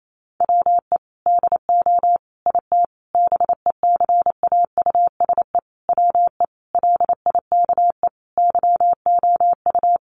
Morse code, heard as a keyed tone, about 28 wpm, 700 Hz. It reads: WE DO IT BECAUSE WE LIKE YOU
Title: Sentences from Top 1400 Words 28wpm